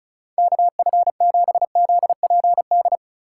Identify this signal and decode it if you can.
Morse code, heard as a keyed tone, about 36 wpm, 690 Hz. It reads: KF7ZPD
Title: USA Call Signs - 6 Characters - 3x Character Spacing - 35wpm